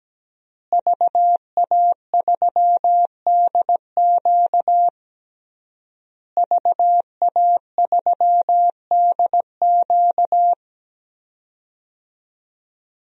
Morse code, heard as a keyed tone, about 17 wpm, 700 Hz. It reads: VA3DQ VA3DQ
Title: DXCC-100 Call Signs - 5 Characters - Repeated 3x - 17wpm